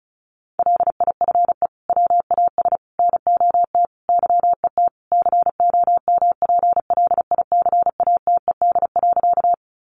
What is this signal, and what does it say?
Morse code, heard as a keyed tone, about 34 wpm, 700 Hz. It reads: LIFE WAS NOT YET COMPLICATED.
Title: Sentences from Top 1200 Words 35wpm